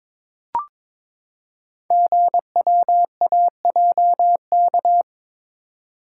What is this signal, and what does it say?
Morse code, heard as a keyed tone, about 22 wpm, 700 Hz. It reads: GWAJK
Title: Code Groups - 5 Letter:Numbers - 22wpm